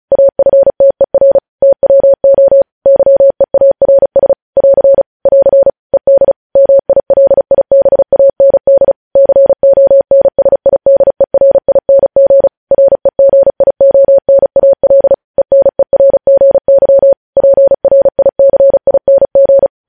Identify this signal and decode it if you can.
Morse code, heard as a keyed tone, about 36 wpm, 550 Hz. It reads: AFTER TWO YEARS <AR> <AR> ED MILIBAND CONSIDERING REGIONAL ENERGY PRICING
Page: news headlines in morse code at 35 wpm